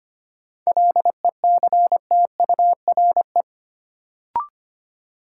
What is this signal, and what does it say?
Morse code, heard as a keyed tone, about 25 wpm, 700 Hz. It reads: LECTURE
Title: Sentences from Top 1600 Words - New Word Drill Alt - 25wpm